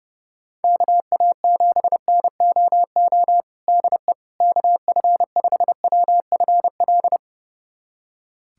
Morse code, heard as a keyed tone, about 32 wpm, 700 Hz. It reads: KA7NOO DE KF5WFL
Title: USA Callsign DE Callsign 30wpm